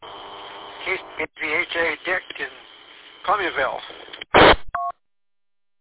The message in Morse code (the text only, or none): none